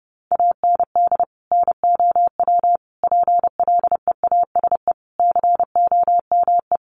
Morse code, heard as a keyed tone, about 30 wpm, 700 Hz. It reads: AND NOW PLEASE COME
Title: Sentences from Top 1500 Words 30wpm